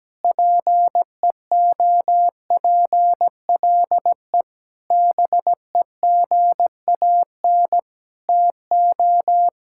Morse code, heard as a keyed tone, about 17 wpm, 700 Hz. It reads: PEOPLE BEGAN TO
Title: Sentences from Top 1600 Words 17wpm